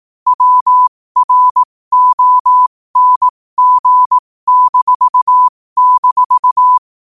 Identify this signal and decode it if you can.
Morse code, heard as a keyed tone, about 18 wpm, spaced as for 15 wpm, 1000 Hz. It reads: WRONG--
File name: qotdfiles / 15wpm